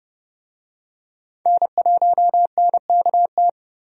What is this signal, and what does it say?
Morse code, heard as a keyed tone, about 31 wpm, 700 Hz. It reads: N1NKT
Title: USA Call Signs - 5 Characters - 30wpm